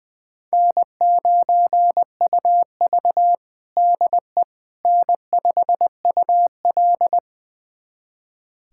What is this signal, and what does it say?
Morse code, heard as a keyed tone, about 20 wpm, 700 Hz. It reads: N9UV DE N5UL
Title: USA Callsign DE Callsign - 4 Characters - 20wpm